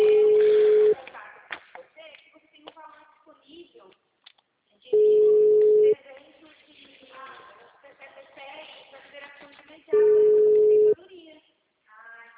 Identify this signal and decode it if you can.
Ringback tone (Europe, 425 Hz), with speech.
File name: out- -2002- - - .86071